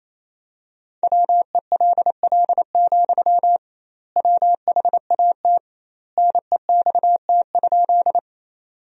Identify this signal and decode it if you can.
Morse code, heard as a keyed tone, about 28 wpm, 700 Hz. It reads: WELL, WHAT NEXT?
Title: 2x Word-Spacing - Sentences from Top 300 Words - 28wpm